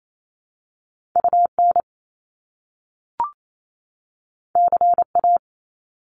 Morse code, heard as a keyed tone, about 28 wpm, 700 Hz. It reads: UN CA
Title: Top Two Letter Combinations in Words 28wpm